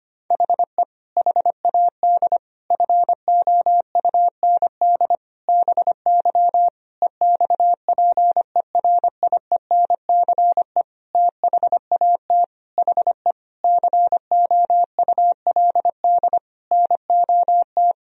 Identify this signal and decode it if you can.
Morse code, heard as a keyed tone, about 25 wpm, 700 Hz. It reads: HE HAD FOUND BY EXPERIENCE THAT HE COULD NOT
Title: Sentences from Top 1000 Words - Review Part 3 - 25wpm